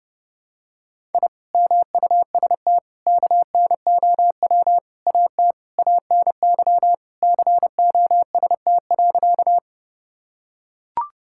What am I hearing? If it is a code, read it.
Morse code, heard as a keyed tone, about 30 wpm, 700 Hz. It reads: I MUST KNOW AT ANY COST.
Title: Sentences from Top 1000 Words - Review Part 5 - 30wpm